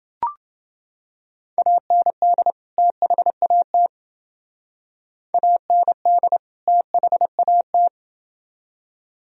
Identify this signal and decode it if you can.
Morse code, heard as a keyed tone, about 28 wpm, 700 Hz. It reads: AND THAT AND THAT